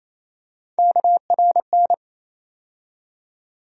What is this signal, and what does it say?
Morse code, heard as a keyed tone, about 28 wpm, 700 Hz. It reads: KRN
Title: USA Call Sign Postfixes - 3 Letters 28wpm